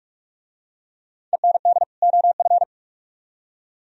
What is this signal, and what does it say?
Morse code, heard as a keyed tone, about 40 wpm, 700 Hz. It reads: END OF